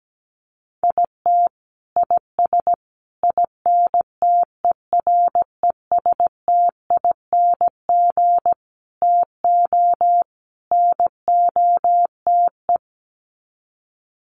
Morse code, heard as a keyed tone, about 17 wpm, 700 Hz. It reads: IT IS INTERESTING TO NOTE
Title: Sets of 5 Words Encore2 17wpm